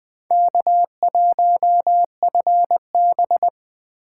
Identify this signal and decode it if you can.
Morse code, heard as a keyed tone, about 20 wpm, 700 Hz. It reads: K1FB
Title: USA Call Signs - 4 Characters - 8x Character Spacing - 20wpm